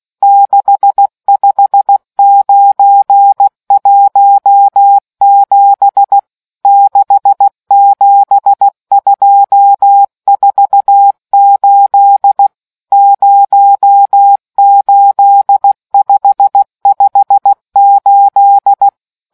Morse code, heard as a keyed tone, about 16 wpm, 800 Hz. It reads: 65917 67248 08558